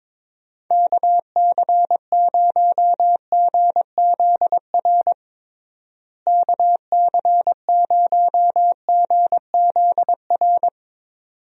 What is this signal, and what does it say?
Morse code, heard as a keyed tone, about 22 wpm, 700 Hz. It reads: KC0GZR KC0GZR
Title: USA Call Signs - 6 Characters - Repeated 2x - 22wpm